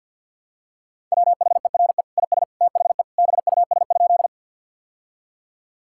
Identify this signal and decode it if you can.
Morse code, heard as a keyed tone, about 50 wpm, 700 Hz. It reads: WHERE IS THE BUS?
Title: Sentences from Top 1000 Words - Review Part 6 - 50wpm